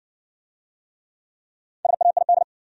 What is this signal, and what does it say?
Morse code, heard as a keyed tone, about 60 wpm, 700 Hz. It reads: SAID